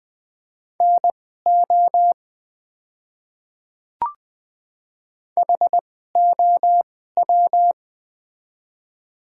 Morse code, heard as a keyed tone, about 20 wpm, spaced as for 15 wpm, 700 Hz. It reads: NO HOW
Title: 2x Character Spacing - Top 100 Words - 20wpm